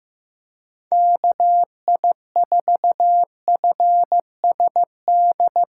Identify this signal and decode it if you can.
Morse code, heard as a keyed tone, about 15 wpm, 700 Hz. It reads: KI4FSD
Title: USA Call Signs - 6 Characters - 4x Character Spacing - 15wpm